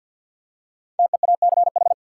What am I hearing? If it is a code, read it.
Morse code, heard as a keyed tone, about 53 wpm, 700 Hz. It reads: TEACH
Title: Five Letter Words - 50wpm